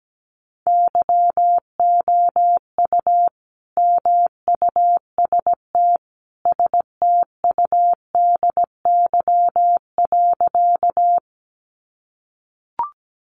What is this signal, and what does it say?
Morse code, heard as a keyed tone, about 17 wpm, 700 Hz. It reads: YOU MUST STUDY.
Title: Sentences from Top 300 Words 17wpm